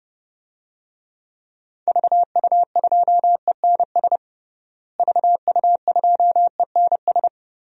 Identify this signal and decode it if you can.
Morse code, heard as a keyed tone, about 30 wpm, 700 Hz. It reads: VU2ENS VU2ENS
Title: DXCC-100 Call Signs - 6 Characters - Repeated 3x - 30wpm